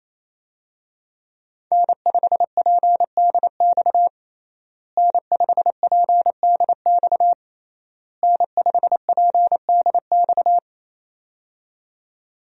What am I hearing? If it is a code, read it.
Morse code, heard as a keyed tone, about 28 wpm, 700 Hz. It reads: N5PDX N5PDX N5PDX
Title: USA Call Signs - 5 Characters - Repeated 3x - 28wpm